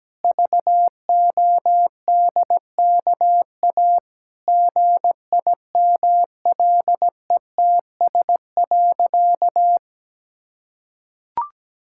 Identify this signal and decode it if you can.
Morse code, heard as a keyed tone, about 17 wpm, 700 Hz. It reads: VODKA GIMLETS.
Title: Pangram Challenge v2 - 17wpm